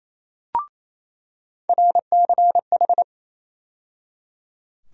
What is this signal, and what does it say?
Morse code, heard as a keyed tone, about 28 wpm, 700 Hz. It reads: RCH
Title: USA Call Sign Postfixes - 3 Letters 28wpm